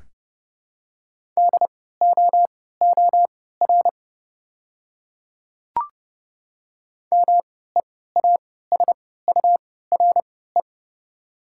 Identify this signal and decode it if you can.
Morse code, heard as a keyed tone, about 30 wpm, spaced as for 17 wpm, 700 Hz. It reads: DOOR MEASURE
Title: 3x Character Spacing - Top 500 Words - 30wpm